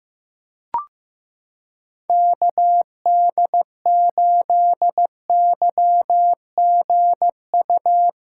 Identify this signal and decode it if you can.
Morse code, heard as a keyed tone, about 15 wpm, 700 Hz. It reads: KD8YGU
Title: USA Callsign DE Callsign - 6 Characters - 15wpm